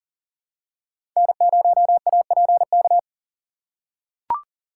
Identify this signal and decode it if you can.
Morse code, heard as a keyed tone, about 40 wpm, 700 Hz. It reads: N0APK
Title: USA Call Signs - 4x Character Spacing - 40wpm